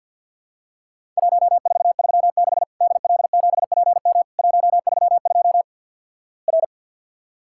Morse code, heard as a keyed tone, about 50 wpm, 700 Hz. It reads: 1436 DL7PM 132 R
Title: Worked All Europe DX Contest - QTC Exchange - 50wpm